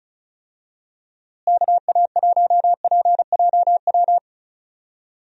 Morse code, heard as a keyed tone, about 34 wpm, 700 Hz. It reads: KA1PJW
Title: USA Call Signs - 6 Characters - Repeated 3x - 35wpm